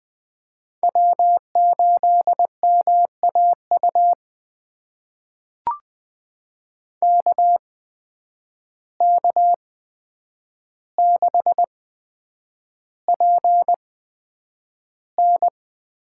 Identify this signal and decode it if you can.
Morse code, heard as a keyed tone, about 20 wpm, spaced as for 9 wpm, 700 Hz. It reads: W8MAU KK6PN
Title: USA Call Signs - 8x Character Spacing - 20wpm